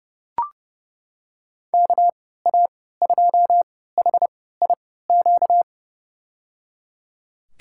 Morse code, heard as a keyed tone, about 29 wpm, spaced as for 17 wpm, 700 Hz. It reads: KA2HIQ
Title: USA Call Signs - 6 Characters - 3x Character Spacing - 30wpm